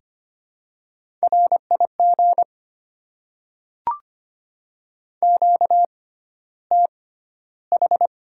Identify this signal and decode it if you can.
Morse code, heard as a keyed tone, about 25 wpm, spaced as for 13 wpm, 690 Hz. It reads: RIG QTH
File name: Top 20 QSO Elements - 6x Character Spacing - 25wpm